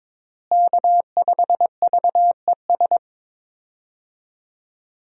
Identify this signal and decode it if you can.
Morse code, heard as a keyed tone, about 22 wpm, 700 Hz. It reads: K5VES
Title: USA Call Signs - 5 Characters - Repeated 3x - 22wpm